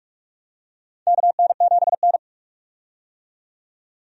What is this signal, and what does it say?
Morse code, heard as a keyed tone, about 45 wpm, 700 Hz. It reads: KN7N
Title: USA Call Signs - 4 Characters - 45wpm